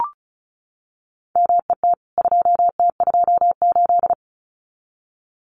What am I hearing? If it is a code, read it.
Morse code, heard as a keyed tone, about 34 wpm, 700 Hz. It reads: MET 2T28
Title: CWT Exchange 35wpm